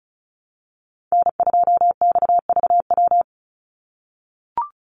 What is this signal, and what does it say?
Morse code, heard as a keyed tone, about 34 wpm, 700 Hz. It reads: N2XVW
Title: USA Call Signs 35wpm